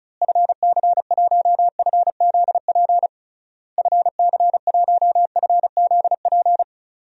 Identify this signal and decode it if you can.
Morse code, heard as a keyed tone, about 36 wpm, 700 Hz. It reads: FC1FZP FC1FZP
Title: DXCC-100 Call Signs - 6 Characters - Repeated 3x - 35wpm